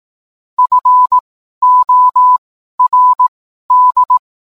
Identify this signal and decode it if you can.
Morse code, heard as a keyed tone, about 18 wpm, spaced as for 12 wpm, 1000 Hz. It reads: FORD